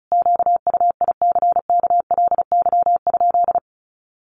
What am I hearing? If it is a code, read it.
Morse code, heard as a keyed tone, about 34 wpm, 700 Hz. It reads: QUICKLY?
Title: Sentences from Top 700 Words 35wpm